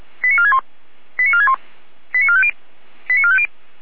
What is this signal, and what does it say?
Five-tone selective call (ZVEI), heard as quick runs of five tones, tones of about 70 ms. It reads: 89461 89461 89450 89450